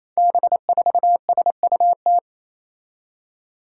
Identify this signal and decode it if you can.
Morse code, heard as a keyed tone, about 28 wpm, 700 Hz. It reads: B4SUT
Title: Five Challenging Letter-Number Combinations 28wpm